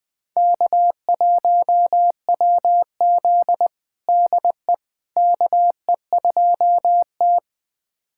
Morse code, heard as a keyed tone, about 21 wpm, 700 Hz. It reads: K1WZ DE KE2T
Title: USA Callsign DE Callsign - 4 Characters - 20wpm